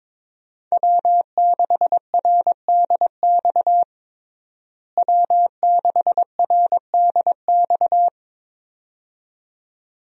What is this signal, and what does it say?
Morse code, heard as a keyed tone, about 22 wpm, 700 Hz. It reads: W6RDX W6RDX
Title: USA Call Signs - 5 Characters - Repeated 2x - 22wpm